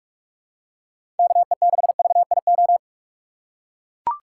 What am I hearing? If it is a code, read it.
Morse code, heard as a keyed tone, about 42 wpm, 690 Hz. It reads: KE6VIO
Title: USA Call Signs - 6 Characters - 3x Character Spacing - 45wpm